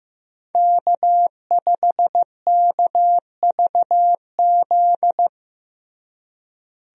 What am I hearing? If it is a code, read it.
Morse code, heard as a keyed tone, about 15 wpm, 700 Hz. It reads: K5KVZ
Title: USA Call Signs - 5 Characters - Repeated 3x - 15wpm